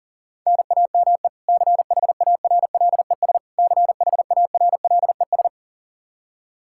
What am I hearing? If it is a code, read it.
Morse code, heard as a keyed tone, about 40 wpm, 700 Hz. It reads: NAME CHARLES CHARLES
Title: Practice QSO - Exchange 1 - Beginning - 40wpm